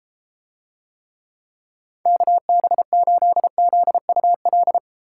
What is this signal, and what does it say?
Morse code, heard as a keyed tone, about 29 wpm, 700 Hz. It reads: KB8ZUL